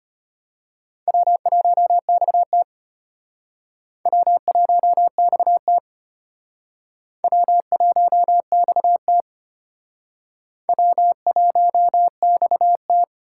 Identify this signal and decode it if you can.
Morse code, heard as a keyed tone, about 32 wpm, 700 Hz. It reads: W1XT W1XT W1XT W1XT